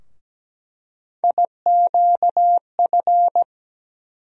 Morse code, heard as a keyed tone, about 17 wpm, 700 Hz. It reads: IQF
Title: USA Call Sign Postfixes - 3 Letters 17wpm